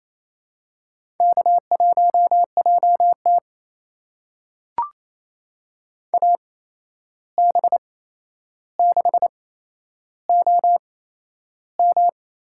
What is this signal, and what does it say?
Morse code, heard as a keyed tone, about 28 wpm, spaced as for 11 wpm, 700 Hz. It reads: K1JT AB6OM
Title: USA Call Signs - 8x Character Spacing - 28wpm